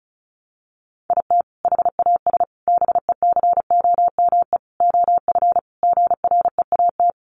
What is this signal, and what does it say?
Morse code, heard as a keyed tone, about 34 wpm, 700 Hz. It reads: IT HAS BECOME OF GREAT
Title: Sentences from Top 1200 Words 35wpm